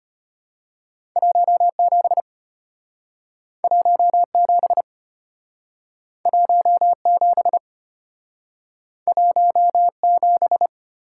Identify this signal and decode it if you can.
Morse code, heard as a keyed tone, about 34 wpm, 700 Hz. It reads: 17 17 17 17